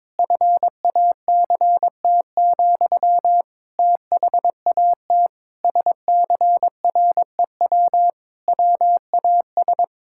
Morse code, heard as a keyed tone, about 22 wpm, 700 Hz. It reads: FACT, THAT SCREW WAS
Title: Sentences from Top 1600 Words 22wpm